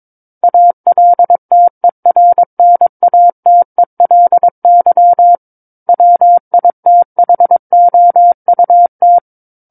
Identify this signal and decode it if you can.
Morse code, heard as a keyed tone, about 22 wpm, 700 Hz. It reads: ALTERNATELY WITHOUT
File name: 06-Chapter-06 22 WPM